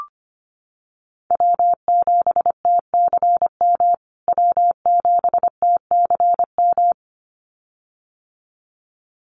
Morse code, heard as a keyed tone, about 25 wpm, 700 Hz. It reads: W7TCM W7TCM
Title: USA Call Signs Repeated 2x 25wpm